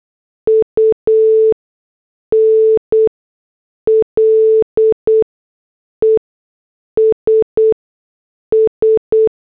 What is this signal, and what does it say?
Morse code, heard as a keyed tone, about 8 wpm, spaced as for 6 wpm, 440 Hz. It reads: UNLESS
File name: morse / headlines